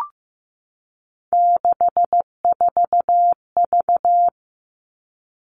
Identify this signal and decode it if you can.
Morse code, heard as a keyed tone, about 15 wpm, 700 Hz. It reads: 64V
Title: All Cut Numbers - 3-Digit 15wpm